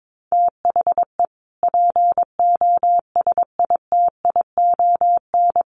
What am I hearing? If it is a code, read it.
Morse code, heard as a keyed tone, about 22 wpm, 700 Hz. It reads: THE POSITION
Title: Sentences from Top 1300 Words 22wpm